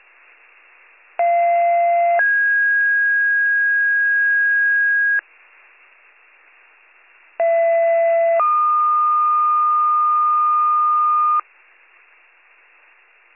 Two-tone page, heard as a one-second tone, then a three-second tone, about 690 then 1650 Hz, about 670 then 1220 Hz. It is heard from 1.2 to 11.4 s.